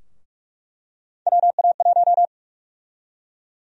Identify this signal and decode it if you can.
Morse code, heard as a keyed tone, about 45 wpm, 700 Hz. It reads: WA1
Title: USA Call Sign Prefixes - 45wpm